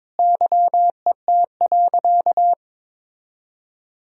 Morse code, heard as a keyed tone, about 22 wpm, 700 Hz. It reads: YET.
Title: Sentences from Top 1000 Words - Review Part 3 - 22wpm